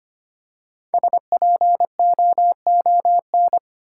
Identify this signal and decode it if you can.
Morse code, heard as a keyed tone, about 25 wpm, 700 Hz. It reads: SPOON
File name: practice / 25wpm / Easy Words with Hints - 25wpm